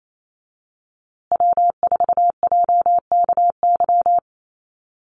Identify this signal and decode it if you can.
Morse code, heard as a keyed tone, about 28 wpm, 700 Hz. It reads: W4JKY
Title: USA Call Signs Repeated 2x 28wpm